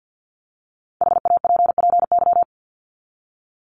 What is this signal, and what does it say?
Morse code, heard as a keyed tone, about 50 wpm, 700 Hz. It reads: HAPPY
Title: Top 751-1000 Words 50wpm